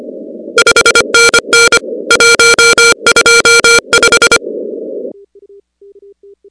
Morse code, heard as a keyed tone, about 25 wpm, 1400 Hz, with noise. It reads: 5NN 125
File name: contests / 2019-cw-wpx-kt5j-k5pi- -2330 0-16